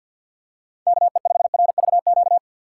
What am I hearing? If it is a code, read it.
Morse code, heard as a keyed tone, about 51 wpm, 700 Hz. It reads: KE5RVQ
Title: USA Call Signs - 6 Characters - 3x Character Spacing - 50wpm